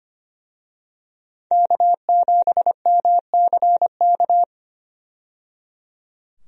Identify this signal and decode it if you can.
Morse code, heard as a keyed tone, about 25 wpm, 700 Hz. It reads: K7MCK
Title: USA Call Signs - 5 Characters - 25wpm